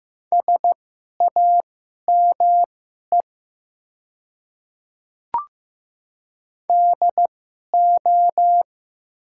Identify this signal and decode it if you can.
Morse code, heard as a keyed tone, about 15 wpm, spaced as for 11 wpm, 700 Hz. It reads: SAME DO
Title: 2x Character Spacing - Top 200 Words - 15wpm